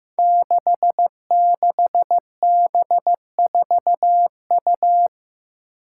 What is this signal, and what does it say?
Morse code, heard as a keyed tone, about 15 wpm, 700 Hz. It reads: 66B4U
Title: Five Challenging Letter-Number Combinations 15wpm